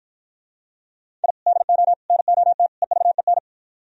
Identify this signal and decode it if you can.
Morse code, heard as a keyed tone, about 50 wpm, 700 Hz. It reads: I DO NOT EVEN